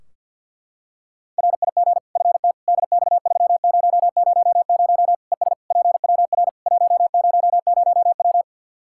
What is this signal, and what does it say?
Morse code, heard as a keyed tone, about 50 wpm, 700 Hz. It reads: RIG FT DX3000 ES PWR 100W
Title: Practice QSO - Exchange 2 - Info - 50wpm